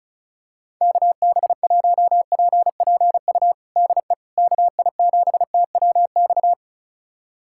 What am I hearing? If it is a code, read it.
Morse code, heard as a keyed tone, about 34 wpm, 700 Hz. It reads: KB1PPU DE KI7TWX
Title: USA Callsign DE Callsign - 6 Characters - 35wpm